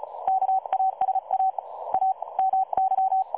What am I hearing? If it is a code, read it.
Morse code, heard as a keyed tone, about 36 wpm, 760 Hz. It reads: KASA A MY
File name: dlazaawansowanych / nagraniazpasma / websdr recording 3565.0kHz